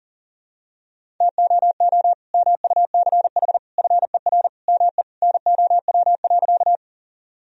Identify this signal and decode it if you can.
Morse code, heard as a keyed tone, about 40 wpm, 700 Hz. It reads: TOO MUCH FER ME NOW.
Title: Sentences from Top 200 Words and Abbreviations 40wpm